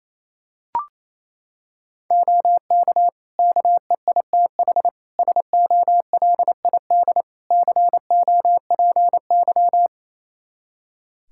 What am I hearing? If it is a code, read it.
Morse code, heard as a keyed tone, about 29 wpm, 700 Hz. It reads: OK KEITH SOLID COPY
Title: Practice QSO - Exchange 3 - More Info - 28wpm